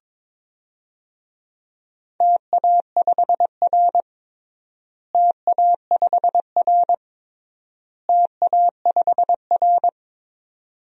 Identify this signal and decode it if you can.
Morse code, heard as a keyed tone, about 23 wpm, 700 Hz. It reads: TA5R TA5R TA5R
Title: DXCC-100 Call Signs - 4 Characters - Repeated 3x - 22wpm